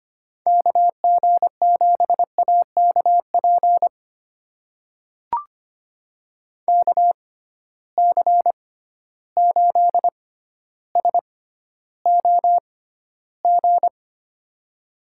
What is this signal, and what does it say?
Morse code, heard as a keyed tone, about 25 wpm, spaced as for 13 wpm, 700 Hz. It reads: KG7AKP KC8SOG
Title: USA Call Signs - 6 Characters - 6x Character Spacing - 25wpm